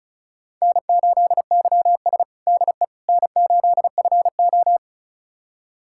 Morse code, heard as a keyed tone, about 34 wpm, 700 Hz. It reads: N8YS DE N8FO
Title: USA Callsign DE Callsign - 4 Characters - 35wpm